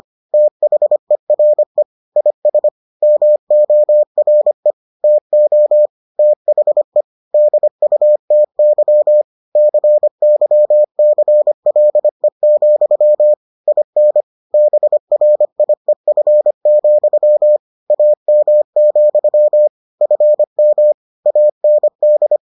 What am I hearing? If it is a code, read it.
Morse code, heard as a keyed tone, about 25 wpm, 600 Hz. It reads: THERE IS MORE TO THE DUTY CYCLE, IN BRIEF, AM, FM AND